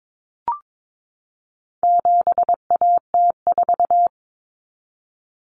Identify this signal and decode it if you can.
Morse code, heard as a keyed tone, about 22 wpm, 700 Hz. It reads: 7AT4
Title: Common Cut Numbers - 4-Digit 22wpm